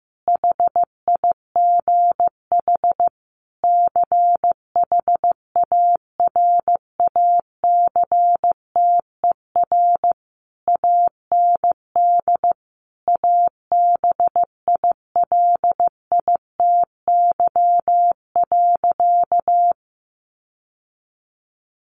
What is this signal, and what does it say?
Morse code, heard as a keyed tone, about 15 wpm, 700 Hz. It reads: HIGH CHARACTER AND ABILITY.
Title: Sentences from Top 1200 Words 15wpm